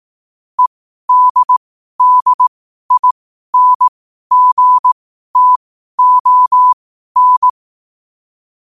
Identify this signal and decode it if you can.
Morse code, heard as a keyed tone, about 18 wpm, spaced as for 12 wpm, 1000 Hz. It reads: EDDINGTON
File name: qotdfiles / 12wpm